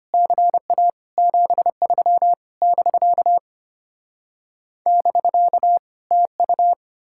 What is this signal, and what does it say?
Morse code, heard as a keyed tone, about 33 wpm, 700 Hz. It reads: CA 73 <BK> <BK> TU